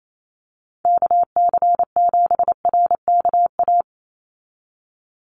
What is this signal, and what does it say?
Morse code, heard as a keyed tone, about 28 wpm, 700 Hz. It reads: KC7RKA
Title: USA Call Signs 28wpm